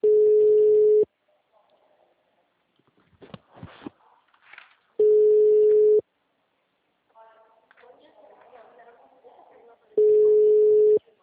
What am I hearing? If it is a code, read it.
Ringback tone (Europe, 425 Hz), with speech.
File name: out- -2016- - - .137